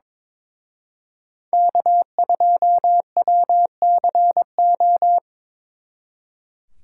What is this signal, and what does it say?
Morse code, heard as a keyed tone, about 25 wpm, 700 Hz. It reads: K2WCO
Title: Speed Racing - USA 1x3 Call Signs 22wpm